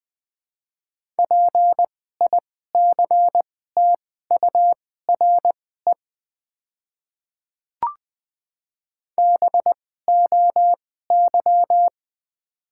Morse code, heard as a keyed tone, about 20 wpm, spaced as for 15 wpm, 700 Hz. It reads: PICTURE BOY
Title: 2x Character Spacing - Top 300 Words - 20wpm